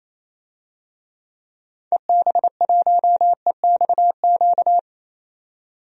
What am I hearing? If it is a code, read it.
Morse code, heard as a keyed tone, about 28 wpm, 700 Hz. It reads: EB1EXQ
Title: DXCC-100 Call Signs - 6 Characters - 28wpm